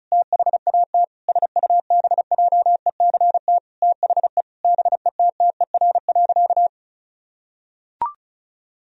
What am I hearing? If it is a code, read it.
Morse code, heard as a keyed tone, about 34 wpm, 700 Hz. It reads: THAT SUBJECT THE BETTER.
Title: Sentences from Top 1000 Words - Review Part 2 - 35wpm